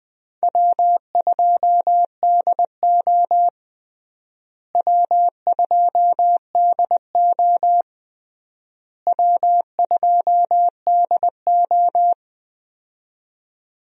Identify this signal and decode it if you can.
Morse code, heard as a keyed tone, about 20 wpm, 700 Hz. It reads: W2DO W2DO W2DO
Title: USA Call Signs - 4 Characters - Repeated 3x - 20wpm